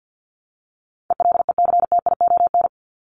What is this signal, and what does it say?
Morse code, heard as a keyed tone, about 50 wpm, 700 Hz. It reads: ELECTION
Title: Top 1251-1500 Words 50wpm